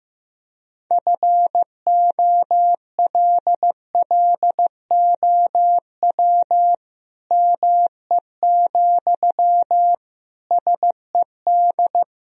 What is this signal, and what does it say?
Morse code, heard as a keyed tone, about 15 wpm, 700 Hz. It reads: FOLLOW ME, SED
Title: Sentences from Top 200 Words and Abbreviations 15wpm